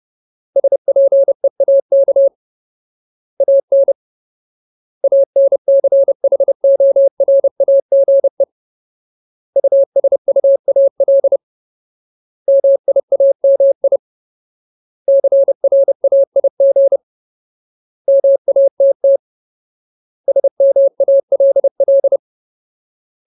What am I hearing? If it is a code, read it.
Morse code, heard as a keyed tone, about 30 wpm, 550 Hz. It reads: SPEAK AN ANCHORAGE USUAL MIAMI CRAIG MATT SMALL
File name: Ultimate Word List / 30WPM Ultimate